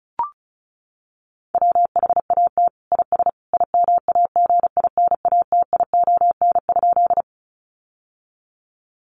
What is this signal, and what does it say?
Morse code, heard as a keyed tone, about 34 wpm, 700 Hz. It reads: WHAT IS IMAGINATION?
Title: Sentences from Top 1300 Words 35wpm